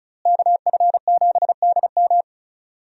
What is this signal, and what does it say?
Morse code, heard as a keyed tone, about 36 wpm, 700 Hz. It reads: KF7DM
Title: USA Call Signs - 5 Characters - Repeated 3x - 35wpm